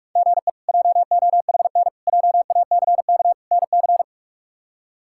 Morse code, heard as a keyed tone, about 43 wpm, 700 Hz. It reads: GE JOHN JACK NC
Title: K1USN SST QSO Practice - 45wpm